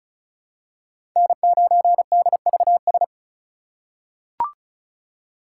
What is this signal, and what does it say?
Morse code, heard as a keyed tone, about 34 wpm, 700 Hz. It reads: N9DVS
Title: USA Call Signs - 5 Characters - 3x Character Spacing - 35wpm